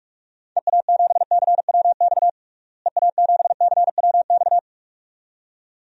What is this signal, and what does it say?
Morse code, heard as a keyed tone, about 46 wpm, 700 Hz. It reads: EA7CWX EA7CWX
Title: DXCC-100 Call Signs - 6 Characters - Repeated 2x - 45wpm